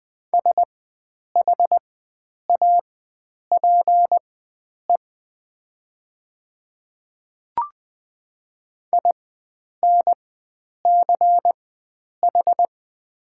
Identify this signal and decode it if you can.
Morse code, heard as a keyed tone, about 20 wpm, spaced as for 9 wpm, 700 Hz. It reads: SHAPE INCH
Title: 4x Character Spacing - Top 500 Words - 20wpm